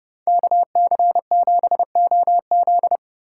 Morse code, heard as a keyed tone, about 35 wpm, 700 Hz. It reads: KC7OZ
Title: Speed Racing - USA 2x2 Call Signs 30wpm